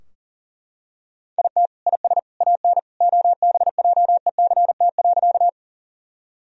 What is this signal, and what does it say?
Morse code, heard as a keyed tone, about 40 wpm, 700 Hz. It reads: IT IS AN OBJECT.
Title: Sentences from Top 1000 Words - Review Part 3 - 40wpm